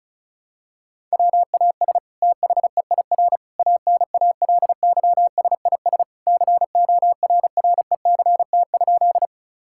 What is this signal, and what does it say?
Morse code, heard as a keyed tone, about 34 wpm, 700 Hz. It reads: WAS THEIR ANALYSIS CORRECT?
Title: Sentences from Top 1000 Words - Review Part 5 - 35wpm